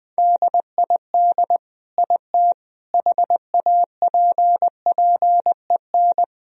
Morse code, heard as a keyed tone, about 20 wpm, 700 Hz. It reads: DID IT HAPPEN
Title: Sentences from Top 1000 Words - Review Part 1 - 20wpm